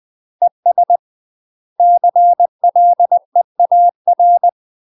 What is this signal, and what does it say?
Morse code, heard as a keyed tone, about 20 wpm, 700 Hz. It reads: ES CLEAR
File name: UK MASTER QSOs TRAINING / 20WPM / UK MASTER QSOs 0044